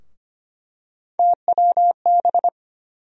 Morse code, heard as a keyed tone, about 25 wpm, 700 Hz. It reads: TWB
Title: USA Call Sign Postfixes - 3 Letters 25wpm